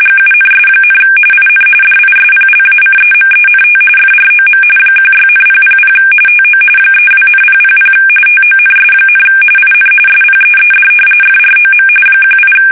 RTTY, about 45.45 baud. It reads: GUNS, IN DISPLACEMENT FROM 16,000 TO 32,000 TONS, IN SPEED FROM 18
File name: fsk850 / battleshipnj